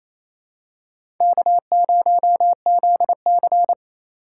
Morse code, heard as a keyed tone, about 28 wpm, 700 Hz. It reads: K0ZC
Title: USA Call Signs - 4 Characters - 2x Character Spacing - 28wpm